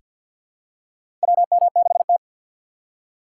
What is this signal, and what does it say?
Morse code, heard as a keyed tone, about 46 wpm, 700 Hz. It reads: WM6T
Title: USA Call Signs - 4 Characters - 3x Character Spacing - 50wpm